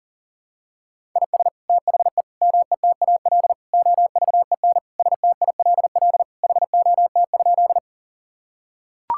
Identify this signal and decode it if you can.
Morse code, heard as a keyed tone, about 40 wpm, 700 Hz. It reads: IS THE METAL OVEN STILL HOT?
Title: Sentences from Top 1000 Words - Review Part 6 - 40wpm